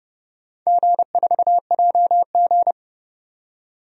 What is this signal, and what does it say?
Morse code, heard as a keyed tone, about 30 wpm, 700 Hz. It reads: G4JG
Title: DXCC-100 Call Signs - 4 Characters - 30wpm